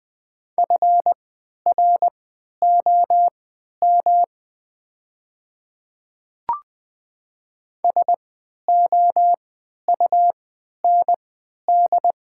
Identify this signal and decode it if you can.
Morse code, heard as a keyed tone, about 20 wpm, spaced as for 11 wpm, 700 Hz. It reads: FROM SOUND
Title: 3x Character Spacing - Top 100 Words - 20wpm